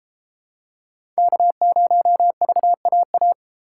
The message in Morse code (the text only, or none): K0VAA